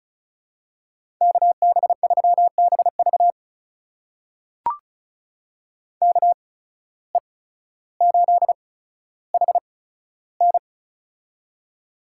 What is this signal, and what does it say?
Morse code, heard as a keyed tone, about 34 wpm, spaced as for 15 wpm, 700 Hz. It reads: KB3BV KE8HN
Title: USA Call Signs - 5 Characters - 8x Character Spacing - 35wpm